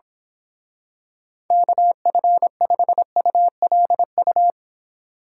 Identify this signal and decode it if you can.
Morse code, heard as a keyed tone, about 25 wpm, 700 Hz. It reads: KF5ULU
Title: Speed Racing - USA 2x3 Call Signs 22wpm